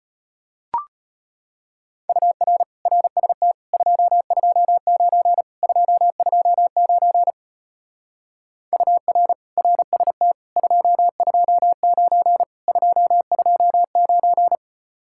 Morse code, heard as a keyed tone, about 33 wpm, 700 Hz. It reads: UR RST 229 229 UR RST 229 229